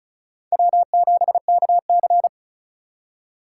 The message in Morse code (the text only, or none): W7KC